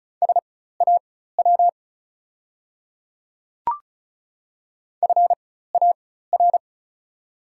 Morse code, heard as a keyed tone, about 36 wpm, spaced as for 16 wpm, 700 Hz. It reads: SAW FAR
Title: 4x Character Spacing - Top 500 Words - 35wpm